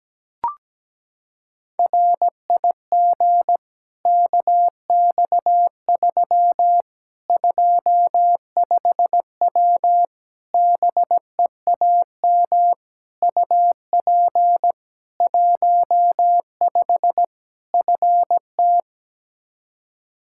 Morse code, heard as a keyed tone, about 17 wpm, 700 Hz. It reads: RIG KX3 25W BEAM UP 15 FT
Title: K1USN SST QSO Practice 17wpm